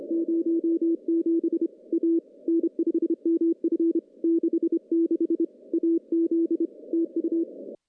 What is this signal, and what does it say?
Morse code, heard as a keyed tone, about 27 wpm, 330 Hz. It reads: 07 A N5MF 66 AZ TU
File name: contests / CW-SS-k5tr- -0400